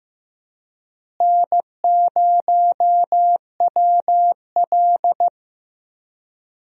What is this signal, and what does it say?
Morse code, heard as a keyed tone, about 15 wpm, 700 Hz. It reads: N0WL
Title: USA Call Signs - 4 Characters - Repeated 2x - 15wpm